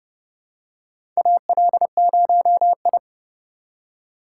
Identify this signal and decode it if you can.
Morse code, heard as a keyed tone, about 30 wpm, 700 Hz. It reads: AL0I
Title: USA Call Signs - 4 Characters - 3x Character Spacing - 30wpm